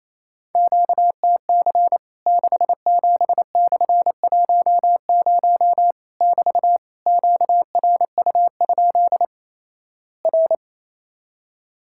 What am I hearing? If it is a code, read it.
Morse code, heard as a keyed tone, about 28 wpm, 700 Hz. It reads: QTC 67/10 <BT> QRU? R
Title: Worked All Europe DX Contest - QTC Exchange - 28wpm